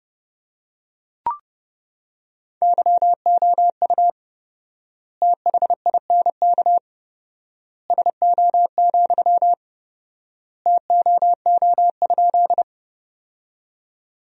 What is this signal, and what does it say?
Morse code, heard as a keyed tone, about 30 wpm, 700 Hz. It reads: YOU THINK SO, TOO?
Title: 4x Word-Spacing - Sentences from Top 200 Words - 30wpm